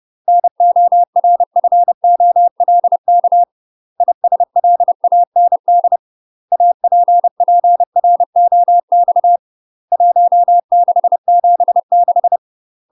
Morse code, heard as a keyed tone, about 30 wpm, 700 Hz. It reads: NORFOLK ISLAND APPROX 1676
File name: UK MASTER QSOs TRAINING / 30WPM / UK MASTER QSOs 0023